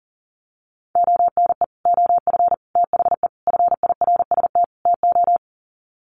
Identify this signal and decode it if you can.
Morse code, heard as a keyed tone, about 40 wpm, 700 Hz. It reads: ONE OF THE FIRST TO
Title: Sets of 5 Words Encore 40wpm